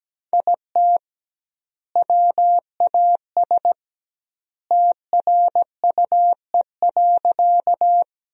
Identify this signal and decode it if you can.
Morse code, heard as a keyed tone, about 17 wpm, 700 Hz. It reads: IT WAS TRUE.
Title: 2x Word-Spacing - Sentences from Top 400 Words - 17wpm